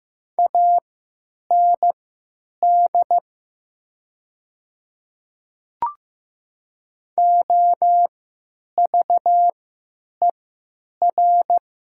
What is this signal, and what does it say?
Morse code, heard as a keyed tone, about 15 wpm, spaced as for 9 wpm, 700 Hz. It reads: AND OVER
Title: 3x Character Spacing - Top 300 Words - 15wpm